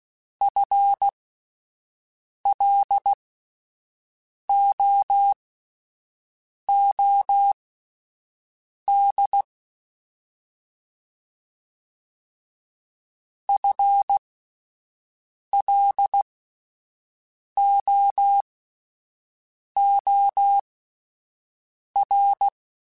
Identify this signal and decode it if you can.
Morse code, heard as a keyed tone, about 16 wpm, spaced as for 5 wpm, 800 Hz. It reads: FLOOD FLOOR
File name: Alphabetical / Alphabetical mp3s / Fs @5